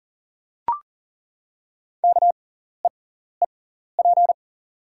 Morse code, heard as a keyed tone, about 39 wpm, spaced as for 14 wpm, 700 Hz. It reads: KEEP
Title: 6x Character Spacing - Top 300 Words - 40wpm